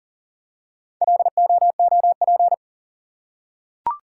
Morse code, heard as a keyed tone, about 40 wpm, 700 Hz. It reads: LOOP
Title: Top 180 QSO Elements 40wpm